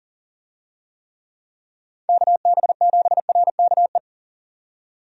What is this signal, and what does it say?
Morse code, heard as a keyed tone, about 40 wpm, 700 Hz. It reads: KB7RKE
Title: USA Call Signs - 6 Characters - 40wpm